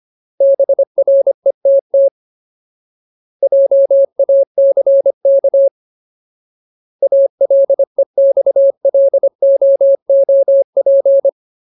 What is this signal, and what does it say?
Morse code, heard as a keyed tone, about 25 wpm, 550 Hz. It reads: BRETT JACK ALEXLOOP
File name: Ultimate Word List / 25WPM Ultimate